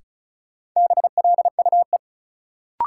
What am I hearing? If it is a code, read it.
Morse code, heard as a keyed tone, about 36 wpm, 700 Hz. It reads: BLUE
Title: Sentences from Top 1200 Words - New Word Drill Alt - 35wpm